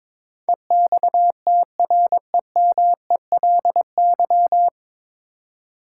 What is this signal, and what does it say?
Morse code, heard as a keyed tone, about 22 wpm, 700 Hz. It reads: EXTREMELY
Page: Morse Code Ninja practice 22wpm